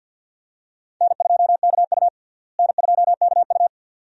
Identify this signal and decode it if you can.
Morse code, heard as a keyed tone, about 52 wpm, 700 Hz. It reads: N2KU N2KU
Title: USA Call Signs - 4 Characters - Repeated 2x - 50wpm